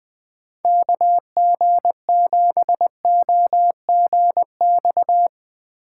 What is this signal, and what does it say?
Morse code, heard as a keyed tone, about 20 wpm, 700 Hz. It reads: KG7OGX
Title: Speed Racing - USA 2x3 Call Signs 17wpm